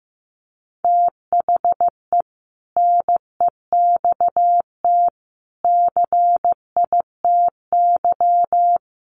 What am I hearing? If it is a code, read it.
Morse code, heard as a keyed tone, about 15 wpm, 700 Hz. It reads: THE NEXT CITY
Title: Sentences from Top 1400 Words 15wpm